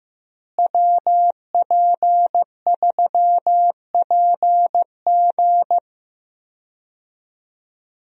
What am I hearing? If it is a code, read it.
Morse code, heard as a keyed tone, about 15 wpm, 700 Hz. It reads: WP3PG
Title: USA Call Signs - 5 Characters - Repeated 3x - 15wpm